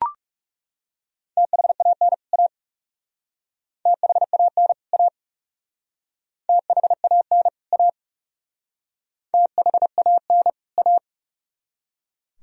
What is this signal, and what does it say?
Morse code, heard as a keyed tone, about 37 wpm, 700 Hz. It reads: THAN A THAN A THAN A THAN A
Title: Speed-Racing - Sets of 2 Words 30wpm